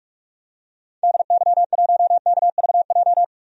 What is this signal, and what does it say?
Morse code, heard as a keyed tone, about 47 wpm, 700 Hz. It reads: DY1KVJ
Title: DXCC-100 Call Signs - 6 Characters - 45wpm